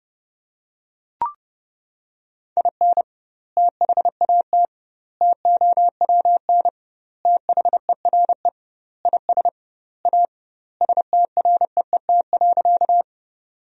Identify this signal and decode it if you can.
Morse code, heard as a keyed tone, about 30 wpm, 700 Hz. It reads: IN THAT TOWN THERE IS A STREET.
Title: 2x Word-Spacing - Sentences from Top 500 Words - 30wpm